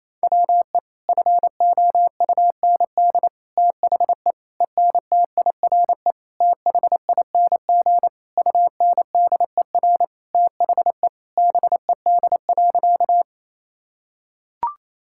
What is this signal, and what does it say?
Morse code, heard as a keyed tone, about 28 wpm, 700 Hz. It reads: WE FOUND THE ENTIRE THING UNDER THE BED.
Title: Sentences from Top 1000 Words - Review Part 5 - 28wpm